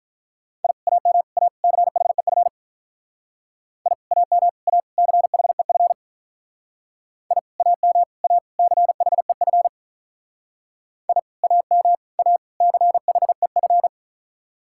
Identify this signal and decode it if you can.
Morse code, heard as a keyed tone, about 44 wpm, 700 Hz. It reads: I AM A CHEF I AM A CHEF I AM A CHEF I AM A CHEF